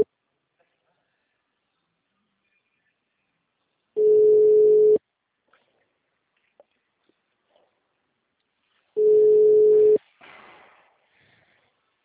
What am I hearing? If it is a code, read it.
Ringback tone (Europe, 425 Hz).